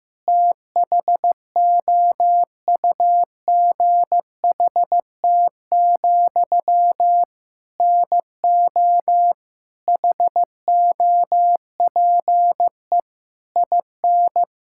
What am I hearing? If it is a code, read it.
Morse code, heard as a keyed tone, about 15 wpm, 700 Hz. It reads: THOUGHT, NO HOPE IN
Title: Sentences from Top 1000 Words - Review Part 6 - 15wpm